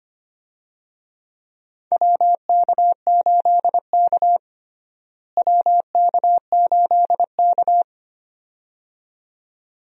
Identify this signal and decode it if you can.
Morse code, heard as a keyed tone, about 25 wpm, 700 Hz. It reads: WK8K WK8K
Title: USA Call Signs - 4 Characters - Repeated 2x - 25wpm